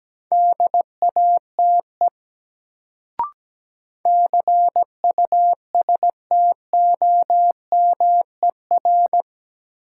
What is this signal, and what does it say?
Morse code, heard as a keyed tone, about 17 wpm, 700 Hz. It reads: DATE CUSTOMER
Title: Sentences from Top 1100 Words - New Word Drill Alt - 17wpm